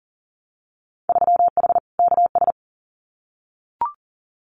Morse code, heard as a keyed tone, about 40 wpm, 700 Hz. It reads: 3H KS
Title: Winter Field Day Exchanges 40wpm